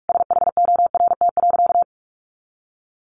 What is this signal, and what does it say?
Morse code, heard as a keyed tone, about 44 wpm, 700 Hz. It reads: SHORT.
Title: Sentences from Top 1400 Words 45wpm